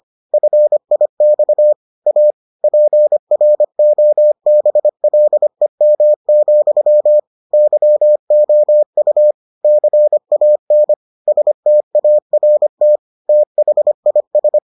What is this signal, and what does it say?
Morse code, heard as a keyed tone, about 25 wpm, 600 Hz. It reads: FIX A PROBLEM, YOU CAN START THIS